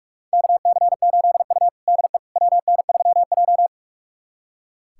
Morse code, heard as a keyed tone, about 44 wpm, 700 Hz. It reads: KC8U DE WN3J
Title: USA Callsign DE Callsign - 4 Characters - 45wpm